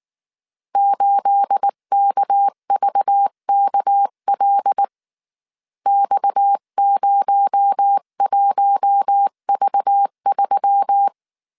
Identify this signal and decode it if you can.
Morse code, heard as a keyed tone, about 19 wpm, 800 Hz. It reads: 8KVKL X01V3